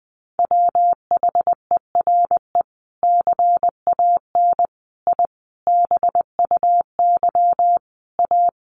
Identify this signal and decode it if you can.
Morse code, heard as a keyed tone, about 20 wpm, 700 Hz. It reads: WHERE CAN I BUY A
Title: Sentences from Top 1500 Words 20wpm